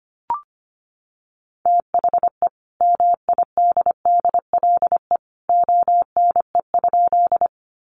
Morse code, heard as a keyed tone, about 25 wpm, 700 Hz. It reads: THE MIDDLE ONE?
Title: Sentences from Top 900 Words 25wpm